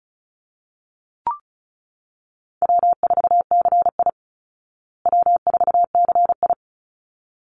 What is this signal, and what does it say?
Morse code, heard as a keyed tone, about 34 wpm, 700 Hz. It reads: W4CI W4CI
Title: Easy Call Signs Repeated 2x - 35wpm